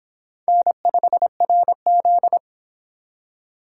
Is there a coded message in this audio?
Yes: Morse code from 0.5 to 2.4 s, heard as a keyed tone, about 31 wpm, 700 Hz. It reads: N5RZ